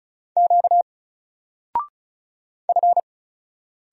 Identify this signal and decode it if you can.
Morse code, heard as a keyed tone, about 36 wpm, 700 Hz. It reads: Q F
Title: Single Letter-Number Alt 35wpm